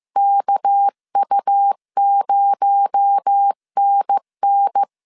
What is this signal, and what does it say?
Morse code, heard as a keyed tone, about 15 wpm, 800 Hz. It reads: KU0NN
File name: CW / DET / 1 9 1500C S A 14wpm 3222